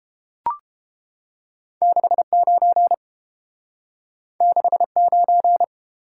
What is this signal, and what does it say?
Morse code, heard as a keyed tone, about 29 wpm, 700 Hz. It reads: 69 69